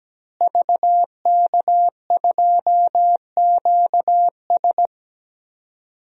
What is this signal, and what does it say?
Morse code, heard as a keyed tone, about 17 wpm, 700 Hz. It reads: VK2QS
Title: DXCC-100 Call Signs - 5 Characters - Repeated 2x - 17wpm